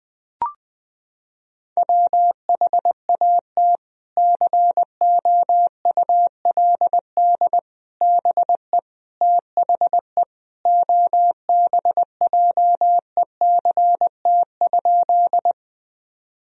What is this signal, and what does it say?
Morse code, heard as a keyed tone, about 20 wpm, 700 Hz. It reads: WHAT COULD BE THE OBJECT?
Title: Sentences from Top 1000 Words - Review Part 3 - 20wpm